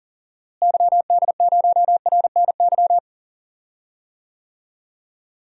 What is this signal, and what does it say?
Morse code, heard as a keyed tone, about 39 wpm, 700 Hz. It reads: YD0RNY
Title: DXCC-100 Call Signs - 6 Characters - Repeated 3x - 40wpm